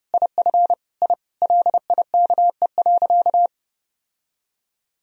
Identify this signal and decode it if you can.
Morse code, heard as a keyed tone, about 30 wpm, 700 Hz. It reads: IF I LIKE.
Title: Sentences from Top 1000 Words - Review Part 1 - 30wpm